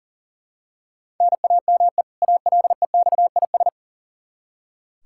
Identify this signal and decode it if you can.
Morse code, heard as a keyed tone, about 40 wpm, 700 Hz. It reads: NAME ALEXIS
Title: Practice QSO - Exchange 1 - Beginning - 40wpm